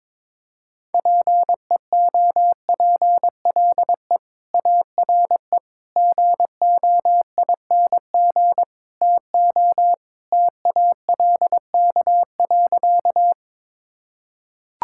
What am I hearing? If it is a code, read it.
Morse code, heard as a keyed tone, about 21 wpm, 700 Hz. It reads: PEOPLE ARE GOING TO TALK.
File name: practice / 22wpm / Sentences with 4-Word Phrases - 22wpm